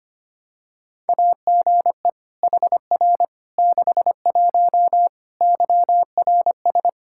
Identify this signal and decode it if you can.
Morse code, heard as a keyed tone, about 25 wpm, 700 Hz. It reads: AGE HR 61 YRS
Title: Practice QSO - Exchange 3 - More Info - 25wpm